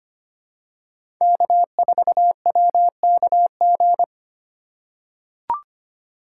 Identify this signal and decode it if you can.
Morse code, heard as a keyed tone, about 25 wpm, 700 Hz. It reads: K4WKG
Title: USA Call Signs - 5 Characters - 3x Character Spacing - 25wpm